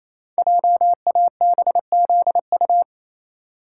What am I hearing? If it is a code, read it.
Morse code, heard as a keyed tone, about 27 wpm, 700 Hz. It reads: JABZU
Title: Code Groups - 5 Letters - 28wpm